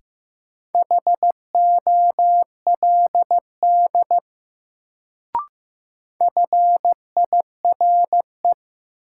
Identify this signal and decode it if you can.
Morse code, heard as a keyed tone, about 15 wpm, 700 Hz. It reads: HOLD FIRE
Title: Sentences from Top 400 Words - New Word Drill - 15wpm